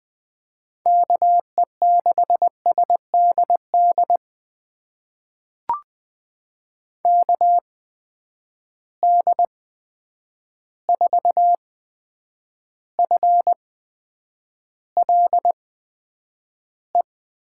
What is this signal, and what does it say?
Morse code, heard as a keyed tone, about 20 wpm, spaced as for 9 wpm, 700 Hz. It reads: KE6SDD KD4FLE
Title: USA Call Signs - 6 Characters - 8x Character Spacing - 20wpm